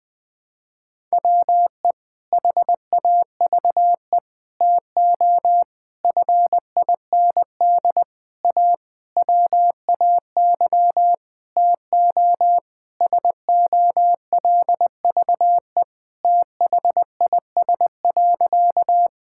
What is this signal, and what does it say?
Morse code, heard as a keyed tone, about 20 wpm, 700 Hz. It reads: WE HAVE TO FIND A WAY TO SOLVE THIS.
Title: Sentences with 6-Word Phrases - Spoken Beginning - 20wpm